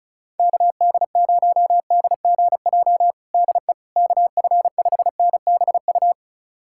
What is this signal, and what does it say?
Morse code, heard as a keyed tone, about 36 wpm, 700 Hz. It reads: KD0DGJ DE KF5NBU
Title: USA Callsign DE Callsign - 6 Characters - 35wpm